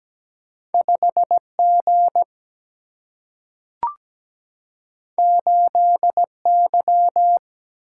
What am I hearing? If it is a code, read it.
Morse code, heard as a keyed tone, about 17 wpm, 700 Hz. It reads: 5 G 8 Y
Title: Single-Letter Single-Number 17wpm